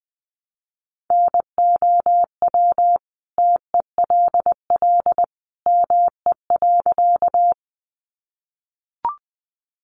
Morse code, heard as a keyed tone, about 20 wpm, 700 Hz. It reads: NOW TELL ME.
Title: Sentences from Top 200 Words 20wpm